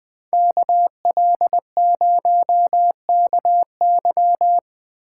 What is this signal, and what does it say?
Morse code, heard as a keyed tone, about 20 wpm, 700 Hz. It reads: KL0KY